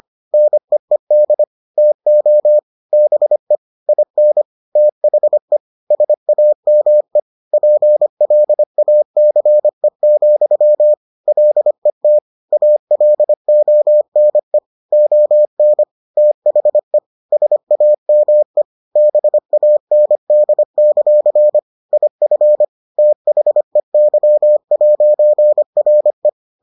Morse code, heard as a keyed tone, about 25 wpm, 600 Hz. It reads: NEED TO BE IN THE SAME PLACE, LET ALONE ON THE SAME BAND; IF THEY'RE